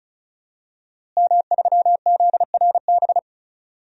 Morse code, heard as a keyed tone, about 36 wpm, 700 Hz. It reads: M3ZRB
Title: DXCC-100 Call Signs - 5 Characters - Repeated 3x - 35wpm